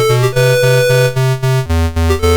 There is no code.